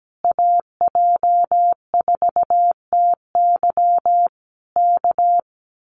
Morse code, heard as a keyed tone, about 17 wpm, 700 Hz. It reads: AJ4TY K
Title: Common QSO Phrases Encore 17wpm